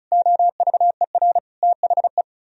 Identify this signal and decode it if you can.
Morse code, heard as a keyed tone, about 46 wpm, spaced as for 41 wpm, 700 Hz. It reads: OVER THE
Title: Speed-Racing - Sets of 5 Words 35wpm